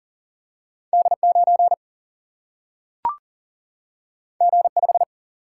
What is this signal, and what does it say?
Morse code, heard as a keyed tone, about 40 wpm, 700 Hz. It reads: D9 G5
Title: Single-Letter Single-Number 40wpm